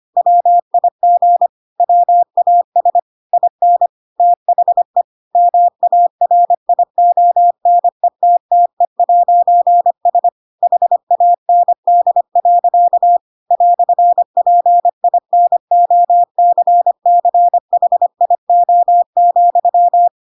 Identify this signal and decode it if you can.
Morse code, heard as a keyed tone, about 25 wpm, 700 Hz. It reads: WIG WAS IN THE MARIONETTE'S HAND. "PINOCCHIO,
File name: Pinocchio / Pinocchio ch3 25 25 ik1hge